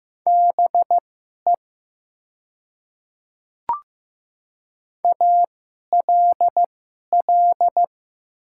Morse code, heard as a keyed tone, about 15 wpm, spaced as for 11 wpm, 700 Hz. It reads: BE ALL
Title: 2x Character Spacing - Top 100 Words - 15wpm